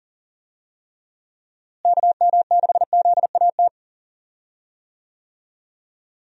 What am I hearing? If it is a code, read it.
Morse code, heard as a keyed tone, about 40 wpm, 700 Hz. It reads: KM6ZAT
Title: USA Call Signs - 6 Characters - Repeated 2x - 40wpm